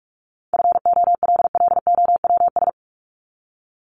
Morse code, heard as a keyed tone, about 44 wpm, 700 Hz. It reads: FOLLOWS
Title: ES:S Suffix - Top 100 Words with Base Words 45wpm